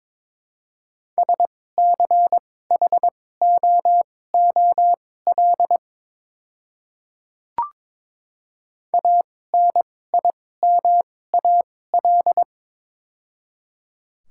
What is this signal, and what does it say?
Morse code, heard as a keyed tone, about 21 wpm, spaced as for 16 wpm, 700 Hz. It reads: SCHOOL ANIMAL
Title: 2x Character Spacing - Top 500 Words - 22wpm